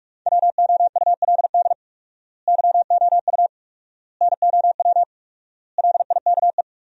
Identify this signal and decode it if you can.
Morse code, heard as a keyed tone, about 46 wpm, 700 Hz. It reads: WOULD YOU NOW LIKE
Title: 4x Word-Spacing - Sentences from Top 200 Words - 45wpm